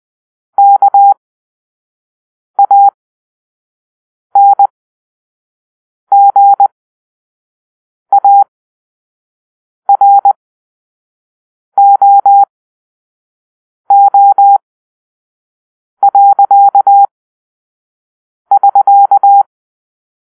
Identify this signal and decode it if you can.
Morse code, heard as a keyed tone, about 20 wpm, spaced as for 5 wpm, 800 Hz. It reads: KANGAROO.<SK>
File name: arrl / Grp-5 Drill-3